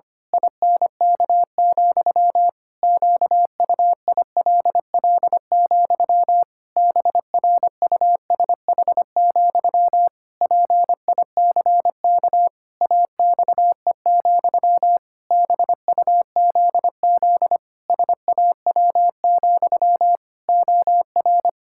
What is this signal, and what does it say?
Morse code, heard as a keyed tone, about 25 wpm, 700 Hz. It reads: INK, QUILL, BRUSH, PICK AXE, BUZZ SAW, OR
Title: Pangram Challenge v2 - 25wpm